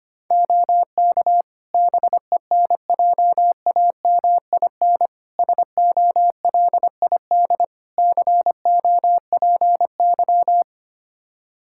Morse code, heard as a keyed tone, about 25 wpm, 700 Hz. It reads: OK BENJAMIN SOLID COPY
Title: Practice QSO - Exchange 3 - More Info - 25wpm